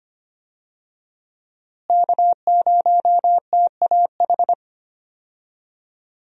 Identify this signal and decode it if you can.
Morse code, heard as a keyed tone, about 25 wpm, 700 Hz. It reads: K0TAH
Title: USA Call Signs - 5 Characters - Repeated 3x - 25wpm